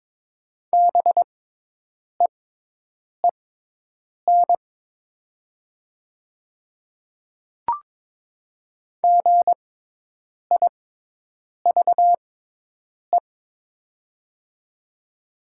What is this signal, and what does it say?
Morse code, heard as a keyed tone, about 23 wpm, spaced as for 8 wpm, 690 Hz. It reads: BEEN GIVE
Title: 6x Character Spacing - Top 500 Words - 22wpm